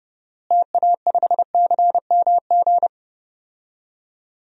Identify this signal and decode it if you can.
Morse code, heard as a keyed tone, about 30 wpm, 700 Hz. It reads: TA5CMG
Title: DXCC-100 Call Signs - 6 Characters - Repeated 2x - 30wpm